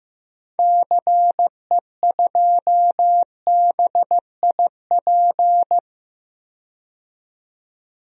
Morse code, heard as a keyed tone, about 15 wpm, 700 Hz. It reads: CE2BIP
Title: DXCC-100 Call Signs - 6 Characters - 15wpm